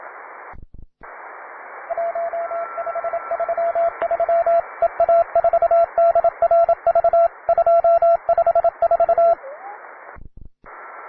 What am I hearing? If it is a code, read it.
Morse code, heard as a keyed tone, about 27 wpm, 680 Hz, with radio noise. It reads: 1533 EA4DRV 254
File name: wae / cw beginner / py2rh cw